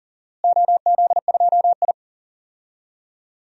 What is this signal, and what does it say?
Morse code, heard as a keyed tone, about 40 wpm, 700 Hz. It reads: OZ2I
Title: DXCC-100 Call Signs - 4 Characters - Repeated 3x - 40wpm